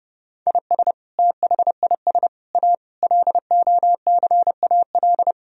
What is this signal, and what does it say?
Morse code, heard as a keyed tone, about 32 wpm, 700 Hz. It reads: IS THIS A LOCAL
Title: Sentences from Top 1000 Words - Review Part 3 - 30wpm